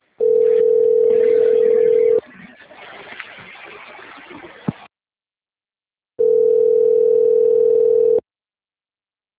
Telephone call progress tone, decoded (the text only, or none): ringback tone